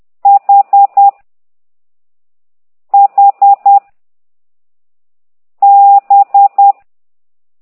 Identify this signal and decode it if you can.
Morse code, heard as a keyed tone, about 10 wpm, 800 Hz. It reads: H H B
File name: ct2idw / exerc9